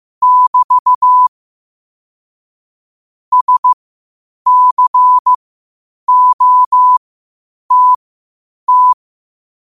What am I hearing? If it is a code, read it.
Morse code, heard as a keyed tone, about 15 wpm, spaced as for 8 wpm, 1000 Hz. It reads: <BT> SCOTT
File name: qotdfiles / 8wpm